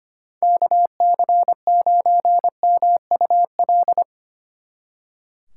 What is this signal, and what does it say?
Morse code, heard as a keyed tone, about 30 wpm, 700 Hz. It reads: KC9MUL
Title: Speed Racing - USA 2x3 Call Signs 25wpm